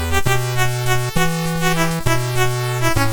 There is no code.